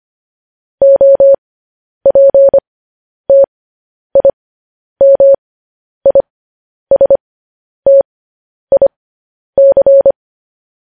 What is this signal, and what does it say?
Morse code, heard as a keyed tone, about 25 wpm, spaced as for 10 wpm, 550 Hz. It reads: OPTIMISTIC
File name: podcast / News headlines-1x-25x10